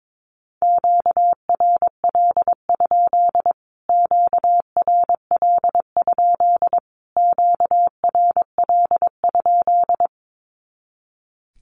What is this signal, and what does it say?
Morse code, heard as a keyed tone, about 22 wpm, 700 Hz. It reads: QRL? QRL? QRL?
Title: Q-Codes Repeated 3x 22wpm